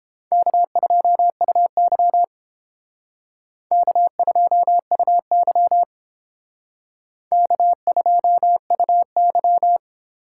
Morse code, heard as a keyed tone, about 32 wpm, 700 Hz. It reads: K2UY K2UY K2UY